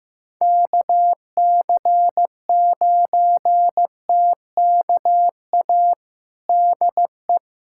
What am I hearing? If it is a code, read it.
Morse code, heard as a keyed tone, about 15 wpm, 700 Hz. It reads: KC9TKA DE
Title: USA Callsign DE Callsign 15wpm